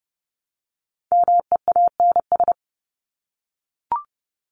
Morse code, heard as a keyed tone, about 30 wpm, 700 Hz. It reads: MEANS
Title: ES:S Suffix - Top 100 Words with Base Words 30wpm